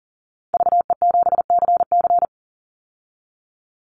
Morse code, heard as a keyed tone, about 39 wpm, 700 Hz. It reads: VE7CC
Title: Call Signs Repeated 2x - 40wpm